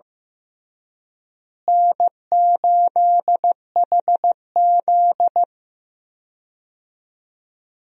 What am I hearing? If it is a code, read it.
Morse code, heard as a keyed tone, about 15 wpm, 700 Hz. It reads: N8HZ
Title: USA Call Signs - 4 Characters - Repeated 3x - 15wpm